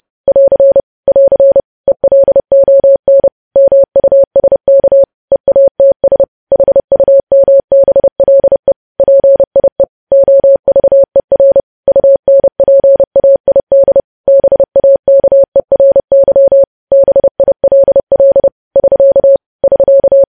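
Morse code, heard as a keyed tone, about 30 wpm, 550 Hz. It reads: <AR> <AR> ELON MUSK EATS HUMBLE PIE OVER UNPAID BAKERY BILL <SK> <SK>
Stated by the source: Morse code news headlines sent at 30, preceded by vvv vvv.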